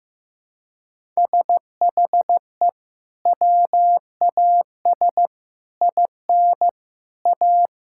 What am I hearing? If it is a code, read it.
Morse code, heard as a keyed tone, about 15 wpm, 700 Hz. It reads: SHE WAS IN A
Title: Sentences from Top 1000 Words - Review Part 5 - 15wpm